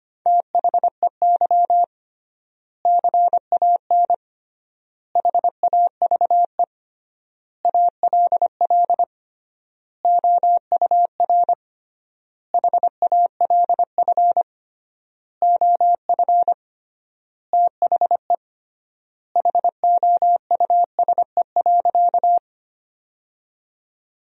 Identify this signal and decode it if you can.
Morse code, heard as a keyed tone, about 25 wpm, 700 Hz. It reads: THEY CAN HAVE ALL OUR HALF OF THE HOUSE.
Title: 3x Word-Spacing - Sentences from Top 400 Words - 25wpm